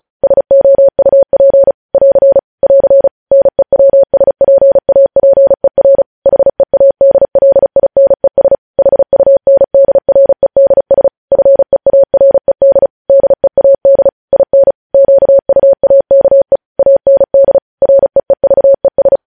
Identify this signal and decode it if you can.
Morse code, heard as a keyed tone, about 36 wpm, 550 Hz. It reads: SOUP <AR> <AR> NEWSPAPER HEADLINES HUNDREDS FEARED DEAD IN QUAKE AND REEVES
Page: News headlines in Morse code at 35 wpm